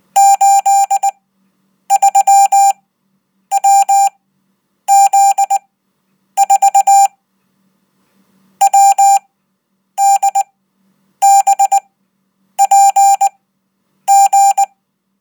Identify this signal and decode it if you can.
Morse code, heard as a keyed tone, about 19 wpm, spaced as for 8 wpm, 790 Hz. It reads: 83WZ4 WDBPG
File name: Code